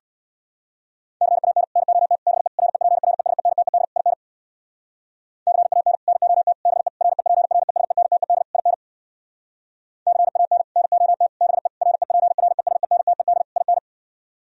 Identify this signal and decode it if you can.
Morse code, heard as a keyed tone, about 64 wpm, 700 Hz. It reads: CAN NOT BE REPRESENTED IN CAN NOT BE REPRESENTED IN CAN NOT BE REPRESENTED IN